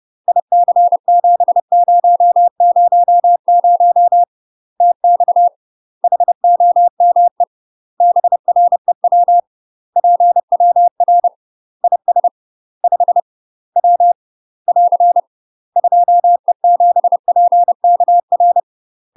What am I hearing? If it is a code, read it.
Morse code, heard as a keyed tone, about 30 wpm, 700 Hz. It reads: IC7000 TX HOME BREW PWR IS 5 W <AR> 2E7PKR
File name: UK MASTER QSOs TRAINING / 30WPM / UK MASTER QSOs 0005